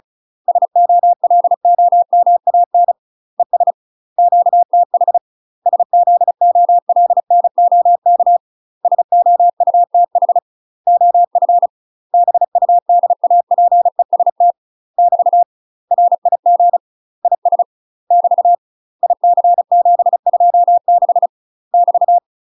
Morse code, heard as a keyed tone, about 34 wpm, 700 Hz. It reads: SOLOMAN ES QTH SZOLNOK SOUTH OF BUDAPEST <BT> RIG IS <BT> IC726 <BT>
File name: UK MASTER QSOs TRAINING / 35WPM / UK MASTER QSOs 0042